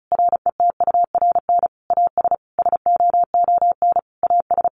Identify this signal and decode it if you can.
Morse code, heard as a keyed tone, about 34 wpm, 700 Hz. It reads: RETURN AS SOON AS
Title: Sentences from Top 800 Words 35wpm